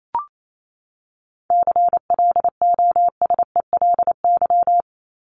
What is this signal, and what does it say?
Morse code, heard as a keyed tone, about 28 wpm, 700 Hz. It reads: CLOSELY
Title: Top 1251-1500 Words 28wpm